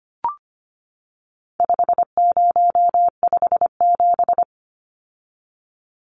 Morse code, heard as a keyed tone, about 25 wpm, 700 Hz. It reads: 5057
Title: Numbers - 4-Digit 25wpm